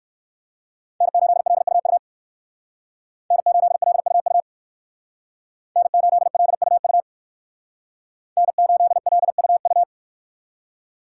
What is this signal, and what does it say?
Morse code, heard as a keyed tone, about 58 wpm, 700 Hz. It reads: N8LUU N8LUU N8LUU N8LUU